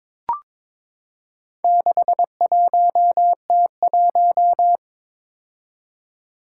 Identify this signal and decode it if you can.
Morse code, heard as a keyed tone, about 23 wpm, 700 Hz. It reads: 61T1
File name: practice / 22wpm / Four Letter-Number Combinations - 22wpm